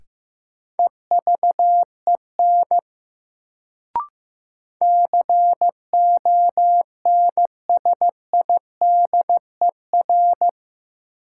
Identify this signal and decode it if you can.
Morse code, heard as a keyed tone, about 15 wpm, 700 Hz. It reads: EVEN CONSIDER
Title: Sentences from Top 600 Words - New Word Drill - 15wpm